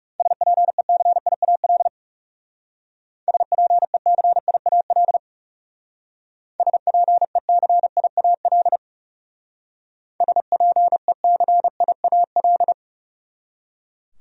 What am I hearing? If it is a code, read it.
Morse code, heard as a keyed tone, about 36 wpm, 700 Hz. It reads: SPECIAL SPECIAL SPECIAL SPECIAL
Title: Speed-Racing Top 500 Words 30wpm